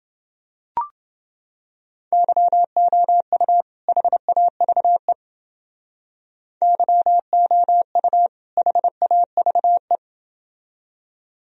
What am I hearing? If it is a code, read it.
Morse code, heard as a keyed tone, about 28 wpm, 700 Hz. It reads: YOU HAVE YOU HAVE